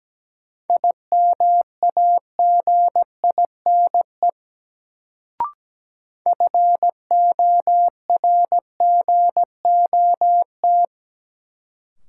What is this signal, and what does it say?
Morse code, heard as a keyed tone, about 17 wpm, 700 Hz. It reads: IMAGINE FORGOT
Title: Sentences from Top 1400 Words - New Word Drill - 17wpm